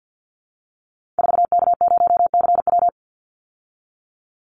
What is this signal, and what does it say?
Morse code, heard as a keyed tone, about 50 wpm, 700 Hz. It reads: 4K0CW
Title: Call Signs Repeated 3x - 50wpm